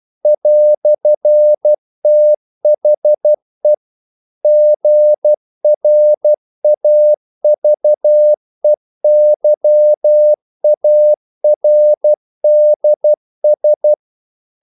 Morse code, heard as a keyed tone, about 12 wpm, 600 Hz. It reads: "THE GRAVEYARDS
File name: CW / Practice / EnglishQuotations0312